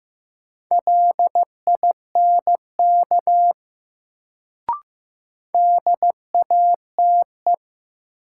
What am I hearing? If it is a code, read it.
Morse code, heard as a keyed tone, about 15 wpm, 700 Hz. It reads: LINK DATE
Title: Sentences from Top 1000 Words - Review Part 7 - New Word Drill - 15wpm